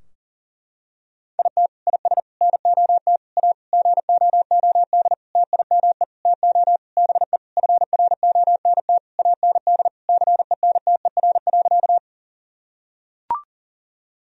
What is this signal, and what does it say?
Morse code, heard as a keyed tone, about 40 wpm, 700 Hz. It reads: IT IS NOT A GOOD TIME TO BE FRONT AND CENTER.
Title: Sentences from Top 1000 Words - Review Part 2 - 40wpm